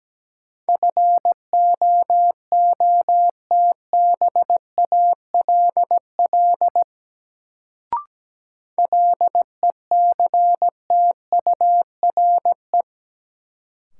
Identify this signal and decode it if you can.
Morse code, heard as a keyed tone, about 17 wpm, 700 Hz. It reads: FOOTBALL LECTURE
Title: Sentences from Top 1600 Words - New Word Drill - 17wpm